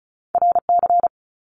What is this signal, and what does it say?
Morse code, heard as a keyed tone, about 36 wpm, 700 Hz. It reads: R C
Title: Top Two Letter Combinations in Words 35wpm